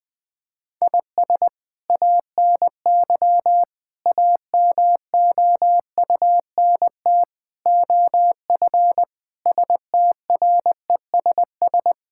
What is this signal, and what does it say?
Morse code, heard as a keyed tone, about 21 wpm, 700 Hz. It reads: IS ANY AMOUNT OF STRESS
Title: Sentences from Top 1000 Words - Review Part 6 - 20wpm